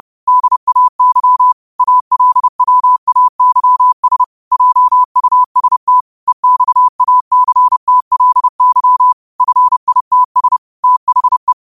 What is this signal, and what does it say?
Morse code, heard as a keyed tone, about 30 wpm, 1000 Hz. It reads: DAY ALWAYS JUST EXACTLY FITS THE
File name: 30wpm